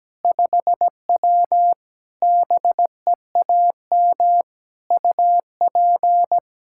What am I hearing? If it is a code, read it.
Morse code, heard as a keyed tone, about 17 wpm, 700 Hz. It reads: 5W BEAM UP
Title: Practice QSO - Exchange 2 - Info - 17wpm